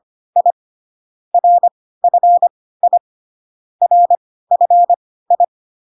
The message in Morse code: I RFI RFI

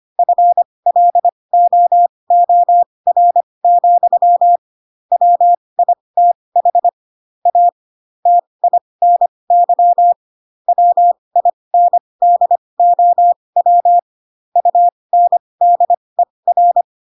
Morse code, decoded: FLOOR, WITH A TINY WINDOW UNDER

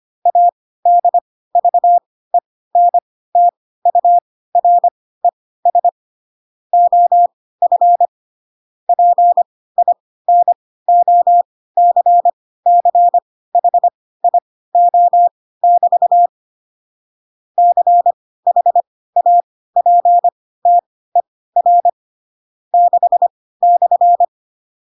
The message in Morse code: ADVENTURES OF PINOCCHIO<BT> CHAPTER 6/

